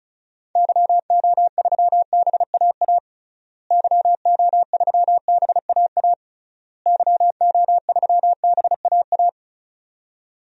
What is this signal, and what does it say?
Morse code, heard as a keyed tone, about 35 wpm, 700 Hz. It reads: YO3BAA YO3BAA YO3BAA
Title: DXCC-100 Call Signs - 6 Characters - Repeated 3x - 35wpm